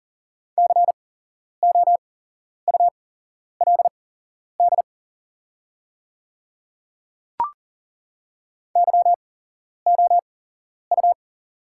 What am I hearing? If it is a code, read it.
Morse code, heard as a keyed tone, about 40 wpm, spaced as for 11 wpm, 700 Hz. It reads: COULD YOU
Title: 8x Character Spacing - Top 100 Words - 40wpm